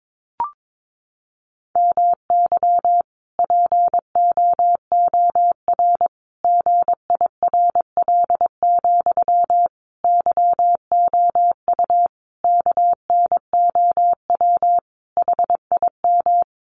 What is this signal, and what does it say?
Morse code, heard as a keyed tone, about 22 wpm, 700 Hz. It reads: MY POOR GIRL, YOU KNOW HIM
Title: Sentences from Top 1000 Words 22wpm